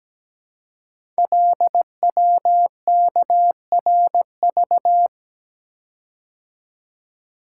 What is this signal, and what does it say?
Morse code, heard as a keyed tone, about 17 wpm, 700 Hz. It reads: LWKRV
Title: Code Groups - 5 Letters - 17wpm